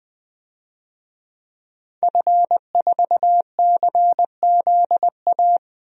Morse code, heard as a keyed tone, about 20 wpm, 700 Hz. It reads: F4CZA
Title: DXCC-100 Call Signs - 5 Characters - 20wpm